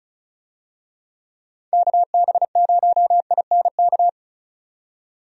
Morse code, heard as a keyed tone, about 34 wpm, 700 Hz. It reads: KB0INK
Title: USA Call Signs - 6 Characters - 35wpm